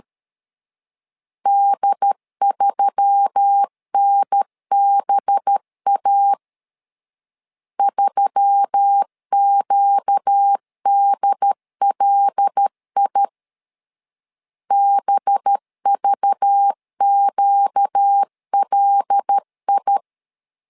Morse code, heard as a keyed tone, about 13 wpm, 800 Hz. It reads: D3NBA 3QDLI BVQLI
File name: LEZIONI / 5 LBVC / 5 LBV 12wpm 1855